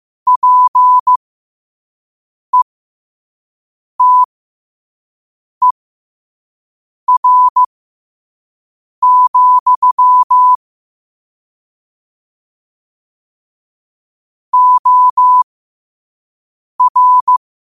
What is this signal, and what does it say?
Morse code, heard as a keyed tone, about 15 wpm, spaced as for 5 wpm, 1000 Hz. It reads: PETER, OR
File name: qotdfiles / 5wpm